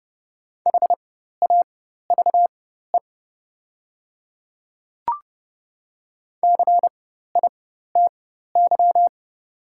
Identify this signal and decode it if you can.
Morse code, heard as a keyed tone, about 30 wpm, spaced as for 14 wpm, 700 Hz. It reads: HAVE CITY
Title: 4x Character Spacing - Top 300 Words - 30wpm